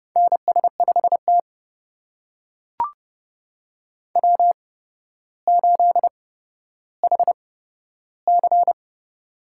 Morse code, heard as a keyed tone, about 30 wpm, spaced as for 13 wpm, 700 Hz. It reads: NS5T W8HC
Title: USA Call Signs - 4 Characters - 8x Character Spacing - 30wpm